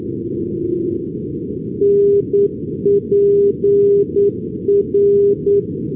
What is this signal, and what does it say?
Morse code, heard as a keyed tone, about 9 wpm, 400 Hz. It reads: NPR
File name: europe / pol / pol-415 np pol npr-488-2015